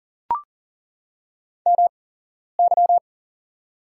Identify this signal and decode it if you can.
Morse code, heard as a keyed tone, about 40 wpm, spaced as for 11 wpm, 700 Hz. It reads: MY
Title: 8x Character Spacing - Top 300 Words - 40wpm